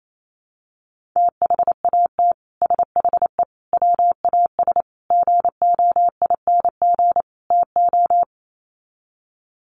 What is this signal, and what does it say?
Morse code, heard as a keyed tone, about 28 wpm, 700 Hz. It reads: THAT SHE WAS GOING TO
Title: Sets of 5 Words Encore2 28wpm